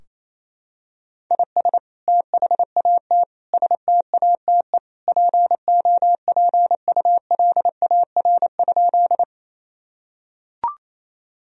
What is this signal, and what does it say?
Morse code, heard as a keyed tone, about 27 wpm, 700 Hz. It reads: IS THAT STATE POPULAR?
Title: Sentences from Top 1000 Words - Review Part 3 - 28wpm